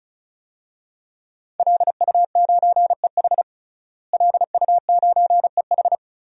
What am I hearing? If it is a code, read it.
Morse code, heard as a keyed tone, about 34 wpm, 700 Hz. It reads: LU9EH LU9EH
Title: DXCC-100 Call Signs - 5 Characters - Repeated 3x - 35wpm